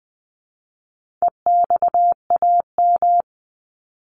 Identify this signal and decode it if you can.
Morse code, heard as a keyed tone, about 20 wpm, 700 Hz. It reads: EXAM
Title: Top 501-750 Words 20wpm